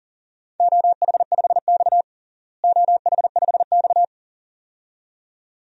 Morse code, heard as a keyed tone, about 40 wpm, 700 Hz. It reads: OH5X OH5X
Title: DXCC-100 Call Signs - 4 Characters - Repeated 2x - 40wpm